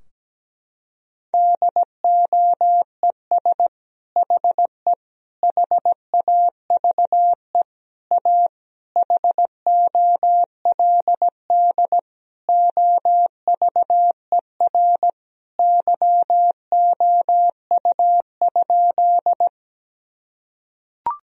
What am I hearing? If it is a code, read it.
Morse code, heard as a keyed tone, about 17 wpm, 700 Hz. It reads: DOES HE HAVE A HOLD OVER YOU?
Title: Sentences from Top 1000 Words - Review Part 2 - 17wpm